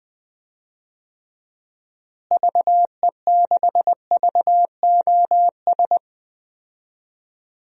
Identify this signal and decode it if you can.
Morse code, heard as a keyed tone, about 20 wpm, 700 Hz. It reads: VE6VOS
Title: DXCC-100 Call Signs - 6 Characters - Repeated 2x - 20wpm